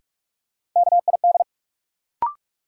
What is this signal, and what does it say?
Morse code, heard as a keyed tone, about 43 wpm, 700 Hz. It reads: KID
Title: Sets of 5 Words - New Word Drill - 45wpm